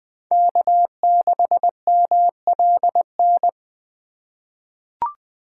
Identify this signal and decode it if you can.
Morse code, heard as a keyed tone, about 21 wpm, 700 Hz. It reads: K6MLN
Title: USA Call Signs - 5 Characters - 2x Character Spacing - 20wpm